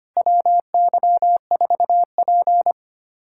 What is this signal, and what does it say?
Morse code, heard as a keyed tone, about 25 wpm, 700 Hz. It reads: WY4P
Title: USA Callsign DE Callsign - 4 Characters - 25wpm